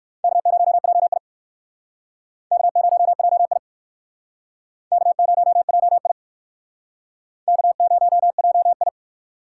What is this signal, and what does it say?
Morse code, heard as a keyed tone, about 56 wpm, 700 Hz. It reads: K0JI K0JI K0JI K0JI
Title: Speed Racing - USA 1x2 Call Signs 45wpm